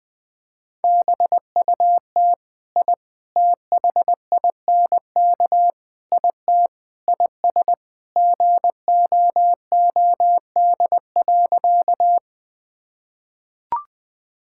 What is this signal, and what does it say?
Morse code, heard as a keyed tone, about 20 wpm, 700 Hz. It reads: BUT I THINK IT IS GOOD.
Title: Sentences with 4-Word Phrases - 20wpm